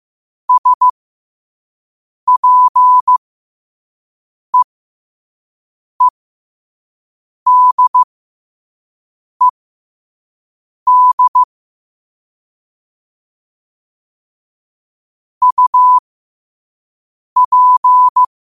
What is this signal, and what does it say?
Morse code, heard as a keyed tone, about 15 wpm, spaced as for 5 wpm, 1000 Hz. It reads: SPEEDED UP
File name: qotdfiles / 5wpm